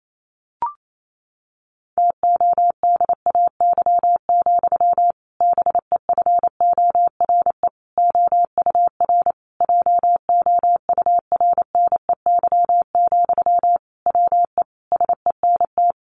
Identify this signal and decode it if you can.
Morse code, heard as a keyed tone, about 28 wpm, 700 Hz. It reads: TODAY, BEFORE OUR JOURNEY, WE SENT
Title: A Sound Of Thunder 28wpm